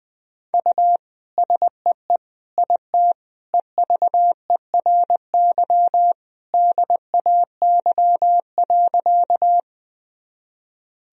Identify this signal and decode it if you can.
Morse code, heard as a keyed tone, about 20 wpm, 700 Hz. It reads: U SEE IT EVERY DAY.
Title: Sentences from Top 200 Words and Abbreviations 20wpm